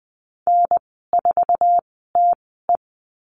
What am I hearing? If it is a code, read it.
Morse code, heard as a keyed tone, about 20 wpm, spaced as for 14 wpm, 700 Hz. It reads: N4TE
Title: 062 Lesson 11g - Letter h - Call Signs 20wpm